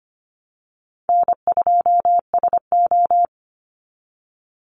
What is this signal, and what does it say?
Morse code, heard as a keyed tone, about 25 wpm, 700 Hz. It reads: N2SO
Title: Easy Call Signs Repeated 2x - 25wpm